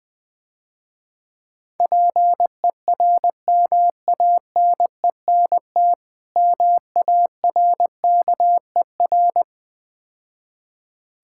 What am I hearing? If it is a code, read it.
Morse code, heard as a keyed tone, about 20 wpm, 700 Hz. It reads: PERMANENT MARKER
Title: Easy Words - Speed-Racing with Hints - 20wpm